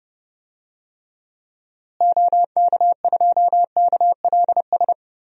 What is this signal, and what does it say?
Morse code, heard as a keyed tone, about 30 wpm, 700 Hz. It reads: OK2KLS
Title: DXCC-100 Call Signs - 6 Characters - Repeated 3x - 30wpm